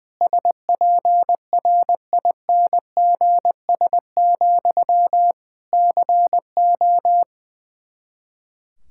Morse code, heard as a keyed tone, about 21 wpm, 700 Hz. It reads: SPRINGS, CO
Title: Practice QSO - Exchange 1 - Beginning - 20wpm